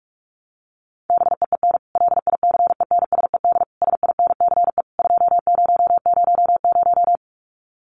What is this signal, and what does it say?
Morse code, heard as a keyed tone, about 44 wpm, 700 Hz. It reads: BEEN LICENSED SINCE 2000
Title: Common QSO Phrases 45wpm